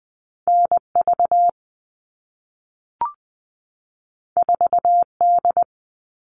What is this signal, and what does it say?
Morse code, heard as a keyed tone, about 20 wpm, 700 Hz. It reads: N V 4 D
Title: All Cut Numbers - 2-Digit 20wpm